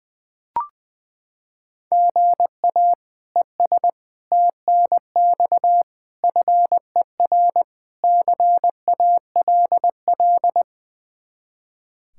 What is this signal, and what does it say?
Morse code, heard as a keyed tone, about 20 wpm, 700 Hz. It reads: GA ES TNX FER CALL
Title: Practice QSO - Exchange 1 - Beginning - 20wpm